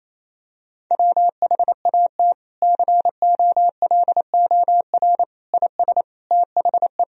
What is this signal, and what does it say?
Morse code, heard as a keyed tone, about 27 wpm, 700 Hz. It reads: WHAT COLOR IS THE
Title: Sentences from Top 1000 Words - Review Part 2 - 28wpm